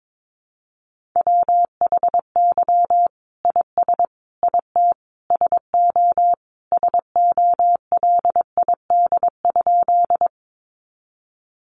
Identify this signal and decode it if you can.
Morse code, heard as a keyed tone, about 22 wpm, 700 Hz. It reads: WHY IS IT SO SOLID?
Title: Sentences from Top 1500 Words 22wpm